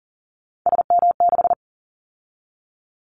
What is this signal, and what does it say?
Morse code, heard as a keyed tone, about 39 wpm, 700 Hz. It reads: SM6
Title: Callsign Prefixes for Top 25 DXCC - 40wpm